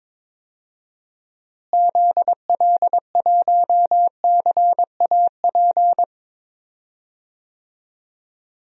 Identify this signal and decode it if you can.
Morse code, heard as a keyed tone, about 21 wpm, 700 Hz. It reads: ZL1CAP
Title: DXCC-100 Call Signs - 6 Characters - Repeated 2x - 22wpm